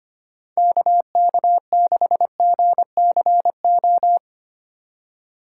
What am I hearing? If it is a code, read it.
Morse code, heard as a keyed tone, about 25 wpm, 700 Hz. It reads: KK6GCO
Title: USA Call Signs - 6 Characters - 4x Character Spacing - 25wpm